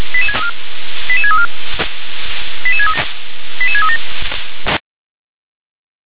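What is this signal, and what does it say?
Five-tone selective call (ZVEI), heard as quick runs of five tones, tones of about 70 ms. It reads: 88634 88637
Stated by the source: ZVEI five-tone selective call 88631, received on an SDR.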